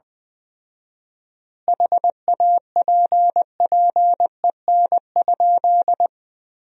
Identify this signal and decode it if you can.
Morse code, heard as a keyed tone, about 21 wpm, 700 Hz. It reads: HAPPEN?
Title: 4x Word-Spacing - Sentences from Top 400 Words - 20wpm